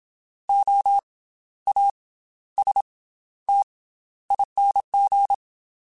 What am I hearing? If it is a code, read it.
Morse code, heard as a keyed tone, about 27 wpm, 800 Hz. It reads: O A S T ING
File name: Special INGs / Special INGs-mp3s / Slow ing @25